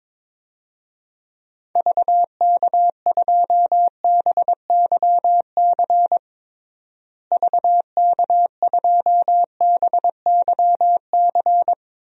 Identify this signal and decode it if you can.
Morse code, heard as a keyed tone, about 22 wpm, 700 Hz. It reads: VK2BYC VK2BYC
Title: DXCC-100 Call Signs - 6 Characters - Repeated 3x - 22wpm